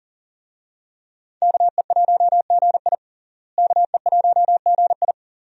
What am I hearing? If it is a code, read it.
Morse code, heard as a keyed tone, about 38 wpm, 700 Hz. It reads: KE1GI KE1GI
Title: USA Call Signs - 5 Characters - Repeated 2x - 40wpm